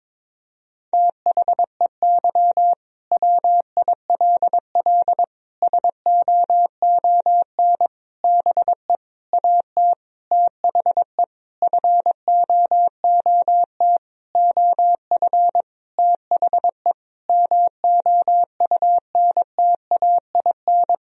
Morse code, heard as a keyed tone, about 22 wpm, 700 Hz. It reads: THEY WILL SOON BE AT THE FOOT OF THE MOUNTAIN
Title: Sentences from Top 1000 Words - Review Part 4 - 22wpm